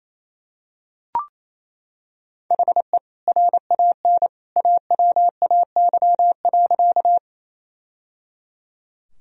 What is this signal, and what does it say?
Morse code, heard as a keyed tone, about 28 wpm, 700 Hz. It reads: HE RAN AWAY.
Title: Sentences from Top 1000 Words - Review Part 3 - 28wpm